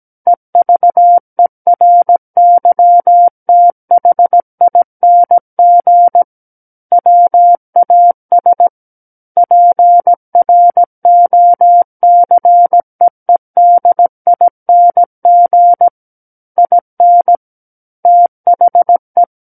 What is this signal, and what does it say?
Morse code, heard as a keyed tone, about 17 wpm, 700 Hz. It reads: EVERYTHING WAS PROCEEDING IN THE
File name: Book One / 08-Chapter 8 - 17 WPM